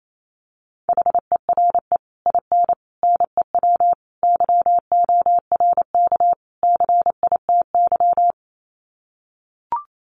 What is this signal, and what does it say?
Morse code, heard as a keyed tone, about 28 wpm, 700 Hz. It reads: HERE IN NEW YORK CITY
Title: Sets of 5 Words Encore2 28wpm